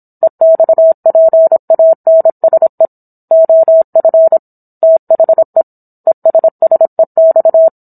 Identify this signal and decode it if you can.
Morse code, heard as a keyed tone, about 27 wpm, 650 Hz. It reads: EXPANSE OF THE ESSEX